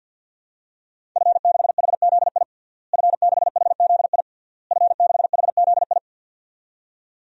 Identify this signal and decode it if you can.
Morse code, heard as a keyed tone, about 51 wpm, 700 Hz. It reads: F6HZI F6HZI F6HZI
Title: DXCC-100 Call Signs - 5 Characters - Repeated 3x - 50wpm